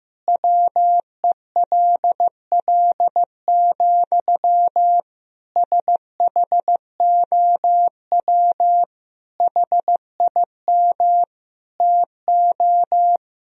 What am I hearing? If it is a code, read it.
Morse code, heard as a keyed tone, about 15 wpm, 700 Hz. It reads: WELL, SHOW HIM TO
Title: Sentences from Top 200 Words and Abbreviations 15wpm